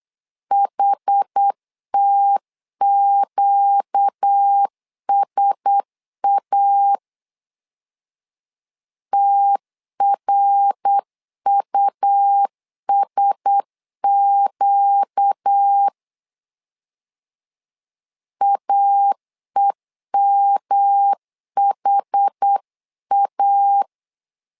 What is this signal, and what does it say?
Morse code, heard as a keyed tone, about 8 wpm, 800 Hz. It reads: HTQSA TRUSQ AEMHA
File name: LEZIONI / 4 Q1S 8wpm 2710